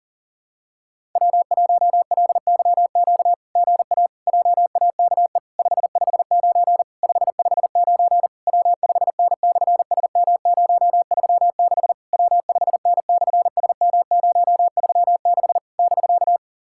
Morse code, heard as a keyed tone, about 38 wpm, 700 Hz. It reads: W1LYQ GA JAKE 559 559 W5N/SM036 W5N/SM036 <BK>
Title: SOTA QSO Practice 40wpm